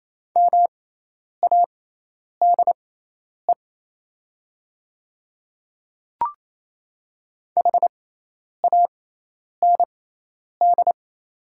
Morse code, heard as a keyed tone, about 27 wpm, spaced as for 10 wpm, 700 Hz. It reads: MADE HAND
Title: 6x Character Spacing - Top 200 Words - 28wpm